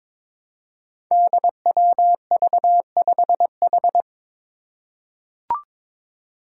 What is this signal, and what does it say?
Morse code, heard as a keyed tone, about 22 wpm, 700 Hz. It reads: DWV5H
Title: Five Challenging Letter-Number Combinations 22wpm